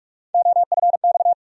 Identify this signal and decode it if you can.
Morse code, heard as a keyed tone, about 42 wpm, 700 Hz. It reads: OFX
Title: USA Call Sign Postfixes - 3 Letters 45wpm